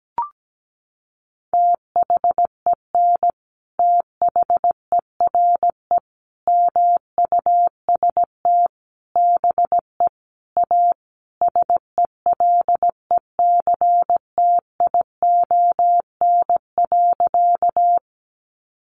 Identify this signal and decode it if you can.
Morse code, heard as a keyed tone, about 17 wpm, 700 Hz. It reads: THEN THERE MUST BE A SELECTION.
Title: Sentences from Top 1400 Words 17wpm